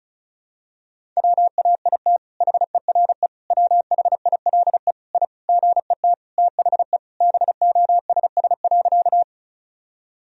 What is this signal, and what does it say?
Morse code, heard as a keyed tone, about 36 wpm, 700 Hz. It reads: WAIT HERE WHILE I GET THE BOSS.
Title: Sentences from Top 1000 Words - Review Part 4 - 35wpm